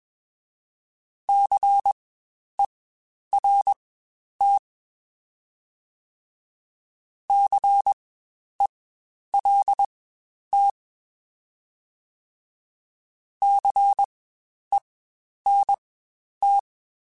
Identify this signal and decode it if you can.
Morse code, heard as a keyed tone, about 20 wpm, spaced as for 10 wpm, 800 Hz. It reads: CERT CELT CENT
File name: Spaced Alphas / Spaced Alphas mp3s / Cs sp @20